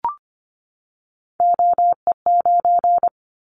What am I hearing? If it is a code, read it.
Morse code, heard as a keyed tone, about 25 wpm, 700 Hz. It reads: OE9
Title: Callsign Prefixes for Top 50 DXCC - 25wpm